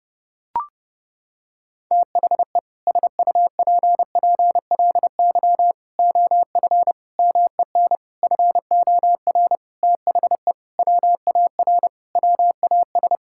Morse code, heard as a keyed tone, about 32 wpm, 700 Hz. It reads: THE SUPPLY OF MEN FOR THE WAR WAS
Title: Sentences from Top 1000 Words - Review Part 6 - 30wpm